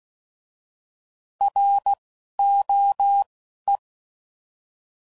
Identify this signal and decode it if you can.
Morse code, heard as a keyed tone, about 16 wpm, 800 Hz. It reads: R O E
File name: Prefixes / By First Letter / Rs